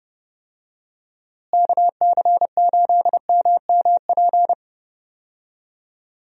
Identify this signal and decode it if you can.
Morse code, heard as a keyed tone, about 31 wpm, 700 Hz. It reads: KC8MMP
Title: USA Call Signs - 6 Characters - Repeated 3x - 30wpm